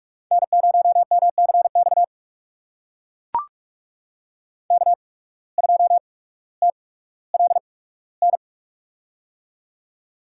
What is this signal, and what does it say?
Morse code, heard as a keyed tone, about 43 wpm, spaced as for 19 wpm, 700 Hz. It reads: N0MCX K2TLN
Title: USA Call Signs - 5 Characters - 8x Character Spacing - 45wpm